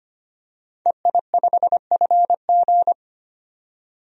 Morse code, heard as a keyed tone, about 25 wpm, 700 Hz. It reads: EI5FG
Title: DXCC-100 Call Signs - 5 Characters - 25wpm